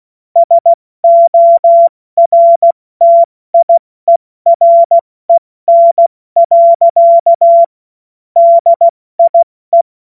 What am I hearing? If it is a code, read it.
Morse code, heard as a keyed tone, about 16 wpm, 680 Hz. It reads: SORTIEREN. DIE
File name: Witz-16-September-2025-70-F80